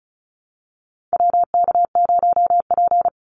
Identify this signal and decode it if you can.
Morse code, heard as a keyed tone, about 34 wpm, 700 Hz. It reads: WK0P
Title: Call Signs - 35wpm